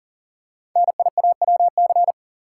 Morse code, heard as a keyed tone, about 40 wpm, 700 Hz. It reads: NIAWC
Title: Code Groups - 5 Letters - 40wpm